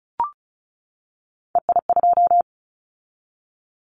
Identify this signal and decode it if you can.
Morse code, heard as a keyed tone, about 36 wpm, 700 Hz. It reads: EI2
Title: Callsign Prefixes for Top 50 DXCC - 35wpm